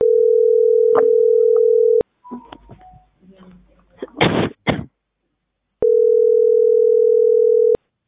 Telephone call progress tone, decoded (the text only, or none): ringback tone